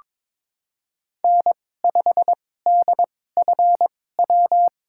Morse code, heard as a keyed tone, about 22 wpm, spaced as for 16 wpm, 700 Hz. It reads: N5DFW
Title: USA Call Signs - 5 Characters - 2x Character Spacing - 22wpm